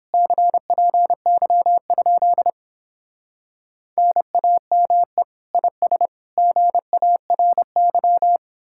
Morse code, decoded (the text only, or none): CPY? NAME IS GARY